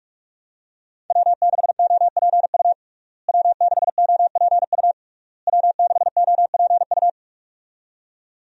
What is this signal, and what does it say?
Morse code, heard as a keyed tone, about 46 wpm, 700 Hz. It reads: W6OPU W6OPU W6OPU
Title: USA Call Signs - 5 Characters - Repeated 3x - 45wpm